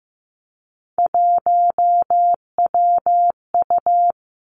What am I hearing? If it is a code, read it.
Morse code, heard as a keyed tone, about 15 wpm, 700 Hz. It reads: 1WU
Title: Three Challenging Letter-Number Combinations 15wpm